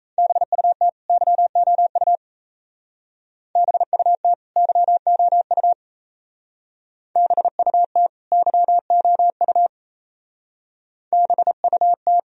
Morse code, heard as a keyed tone, about 33 wpm, 700 Hz. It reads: BUT YOU BUT YOU BUT YOU BUT
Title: Speed-Racing - Sets of 2 Words 28wpm